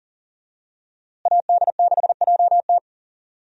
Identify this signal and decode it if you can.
Morse code, heard as a keyed tone, about 39 wpm, 700 Hz. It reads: AD6JT
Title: USA Call Signs - 5 Characters - Repeated 2x - 40wpm